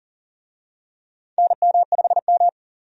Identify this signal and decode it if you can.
Morse code, heard as a keyed tone, about 40 wpm, 700 Hz. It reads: NM5M
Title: USA Call Signs - 4 Characters - 40wpm